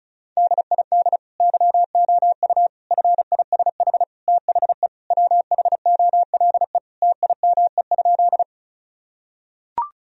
Morse code, heard as a keyed tone, about 34 wpm, 700 Hz. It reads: DID YOU FISH THE WHOLE TIME?
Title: Sentences from Top 1000 Words - Review Part 1 - 35wpm